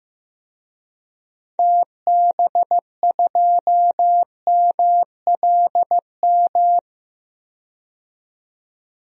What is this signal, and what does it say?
Morse code, heard as a keyed tone, about 15 wpm, 700 Hz. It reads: TB2MLM
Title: DXCC-100 Call Signs - 6 Characters - 15wpm